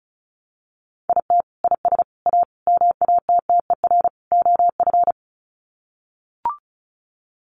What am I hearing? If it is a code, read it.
Morse code, heard as a keyed tone, about 34 wpm, 700 Hz. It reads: IT IS A MATTER OF
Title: Sets of 5 Words Encore2 35wpm